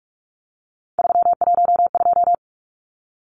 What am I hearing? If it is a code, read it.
Morse code, heard as a keyed tone, about 44 wpm, 700 Hz. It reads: 312
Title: Numbers - 3-Digit 45wpm